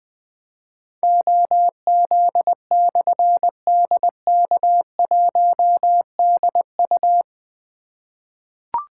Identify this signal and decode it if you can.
Morse code, heard as a keyed tone, about 21 wpm, 700 Hz. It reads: OZ/DK1DU
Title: Difficult Call Signs 20wpm